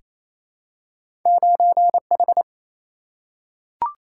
Morse code, heard as a keyed tone, about 28 wpm, 700 Hz. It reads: 9H
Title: Single-Letter Single-Number 28wpm